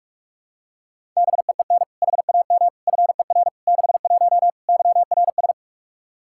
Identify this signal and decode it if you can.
Morse code, heard as a keyed tone, about 44 wpm, 700 Hz. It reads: BEEN HAM FER 61 YRS
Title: K1USN SST QSO Practice 45wpm